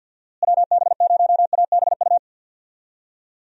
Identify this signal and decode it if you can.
Morse code, heard as a keyed tone, about 52 wpm, 700 Hz. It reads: WB0ABU
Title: USA Call Signs - 6 Characters - Repeated 3x - 50wpm